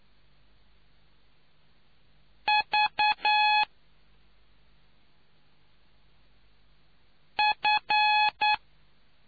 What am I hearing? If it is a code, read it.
Morse code, heard as a keyed tone, about 9 wpm, 840 Hz. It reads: V F